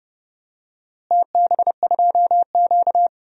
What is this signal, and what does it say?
Morse code, heard as a keyed tone, about 31 wpm, 700 Hz. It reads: TB2Q
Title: DXCC-100 Call Signs - 4 Characters - Repeated 3x - 30wpm